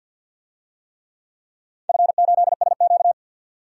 Morse code, heard as a keyed tone, about 46 wpm, 700 Hz. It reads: F8SQ
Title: DXCC-100 Call Signs - 4 Characters - 50wpm